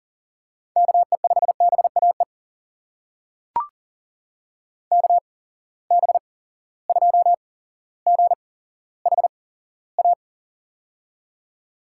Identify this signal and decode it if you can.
Morse code, heard as a keyed tone, about 38 wpm, spaced as for 17 wpm, 700 Hz. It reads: KE5BAE KB2GHA
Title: USA Call Signs - 6 Characters - 8x Character Spacing - 40wpm